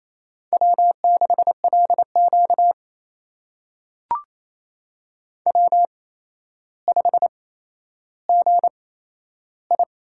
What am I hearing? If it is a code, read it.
Morse code, heard as a keyed tone, about 27 wpm, spaced as for 12 wpm, 700 Hz. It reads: W6LQ W5GI
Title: USA Call Signs - 4 Characters - 8x Character Spacing - 28wpm